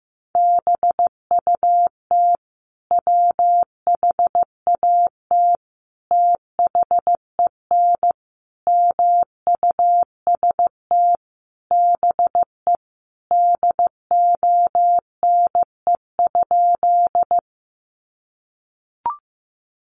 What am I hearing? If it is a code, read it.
Morse code, heard as a keyed tone, about 15 wpm, 700 Hz. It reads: BUT WHAT THEN MUST BE DONE?
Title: Sentences from Top 500 Words 15wpm